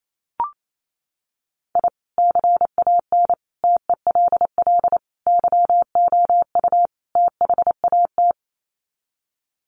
Morse code, heard as a keyed tone, about 28 wpm, 700 Hz. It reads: I CAN TELL YOU THAT
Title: Sets of 5 Words 28wpm